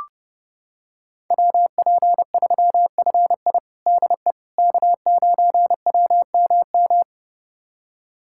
Morse code, heard as a keyed tone, about 32 wpm, 700 Hz. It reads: WP3FI DE K9WMM
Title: USA Callsign DE Callsign - 5 Characters - 30wpm